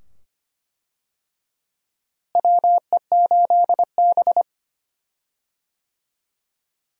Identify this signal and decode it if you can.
Morse code, heard as a keyed tone, about 25 wpm, 700 Hz. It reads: WE8B
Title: USA Call Signs - 4 Characters - Repeated 2x - 25wpm